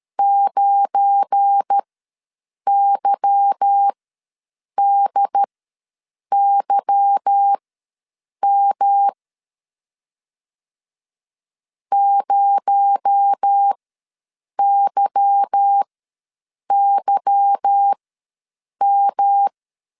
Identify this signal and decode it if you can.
Morse code, heard as a keyed tone, about 13 wpm, spaced as for 7 wpm, 800 Hz. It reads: 9YDYM 0YYM
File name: LEZIONI / 8 YJ45 / 8 Y ws 12wpm 3409